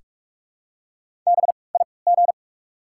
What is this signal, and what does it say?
Morse code, heard as a keyed tone, about 48 wpm, spaced as for 26 wpm, 700 Hz. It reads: BIG
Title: 3x Character Spacing - Top 200 Words - 45wpm